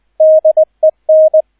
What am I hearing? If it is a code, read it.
Morse code, heard as a keyed tone, about 20 wpm, 630 Hz, with radio noise. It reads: DEN